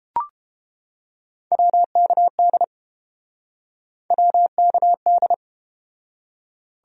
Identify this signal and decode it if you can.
Morse code, heard as a keyed tone, about 31 wpm, 700 Hz. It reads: WKD WKD